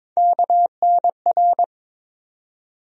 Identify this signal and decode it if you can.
Morse code, heard as a keyed tone, about 22 wpm, 700 Hz. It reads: KNR
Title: USA Call Sign Postfixes - 3 Letters 22wpm